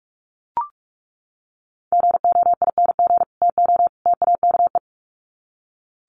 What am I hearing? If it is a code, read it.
Morse code, heard as a keyed tone, about 44 wpm, 700 Hz. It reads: GOING TO TAKE
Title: Sets of 3 Words 45wpm